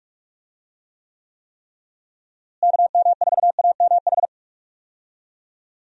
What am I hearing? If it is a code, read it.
Morse code, heard as a keyed tone, about 45 wpm, 700 Hz. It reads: KM4AMH
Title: USA Call Signs - 6 Characters - Repeated 3x - 45wpm